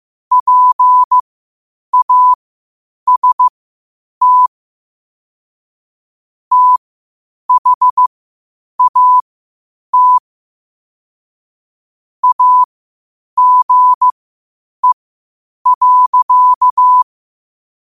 Morse code, heard as a keyed tone, about 15 wpm, spaced as for 8 wpm, 1000 Hz. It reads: PAST THAT AGE.
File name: qotdfiles / 8wpm